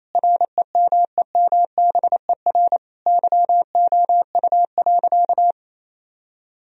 Morse code, heard as a keyed tone, about 28 wpm, 700 Hz. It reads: REMEMBER YOU.
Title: Sentences from Top 1000 Words - Review Part 2 - 28wpm